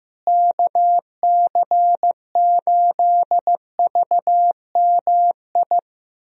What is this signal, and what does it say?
Morse code, heard as a keyed tone, about 16 wpm, 700 Hz. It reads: KC8VMI
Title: USA Call Signs - 6 Characters - Repeated 3x - 15wpm